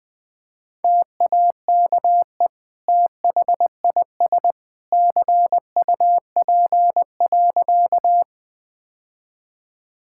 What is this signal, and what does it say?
Morse code, heard as a keyed tone, about 20 wpm, 700 Hz. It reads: TAKE THIS CUP.
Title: Sentences from Top 1600 Words 20wpm